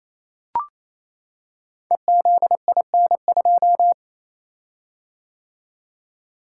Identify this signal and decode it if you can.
Morse code, heard as a keyed tone, about 27 wpm, 700 Hz. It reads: EZIN2
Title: Code Groups - 5 Letter:Numbers - 28wpm